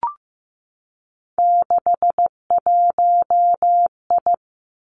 Morse code, heard as a keyed tone, about 15 wpm, 700 Hz. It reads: 61I
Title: Three Challenging Letter-Number Combinations 15wpm